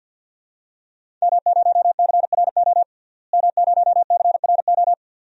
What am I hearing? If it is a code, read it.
Morse code, heard as a keyed tone, about 50 wpm, 700 Hz. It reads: M0CRO M0CRO
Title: DXCC-100 Call Signs - 5 Characters - Repeated 2x - 50wpm